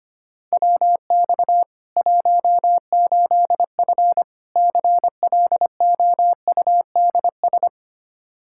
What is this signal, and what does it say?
Morse code, heard as a keyed tone, about 25 wpm, 700 Hz. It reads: WX 18F CLOUDS
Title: Practice QSO - Exchange 2 - Info - 25wpm